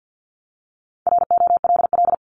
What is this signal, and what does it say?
Morse code, heard as a keyed tone, about 50 wpm, 700 Hz. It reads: ROLL
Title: Multi-letter Phonemes Repeated 3x 50wpm